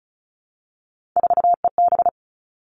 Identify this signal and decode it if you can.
Morse code, heard as a keyed tone, about 34 wpm, 700 Hz. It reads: 4EB
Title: All Cut Numbers - 3-Digit 35wpm